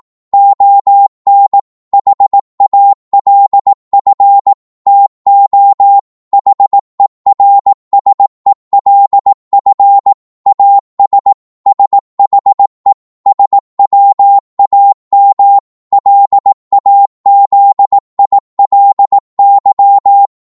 Morse code, heard as a keyed tone, about 18 wpm, 800 Hz. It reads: ON HALF TO HERSELF AS SHE SWAM LAZILY
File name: Morse / psg-18wpm